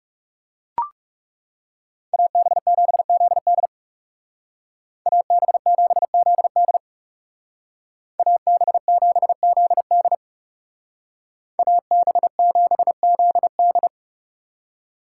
Morse code, heard as a keyed tone, about 38 wpm, 700 Hz. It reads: AB7ZD AB7ZD AB7ZD AB7ZD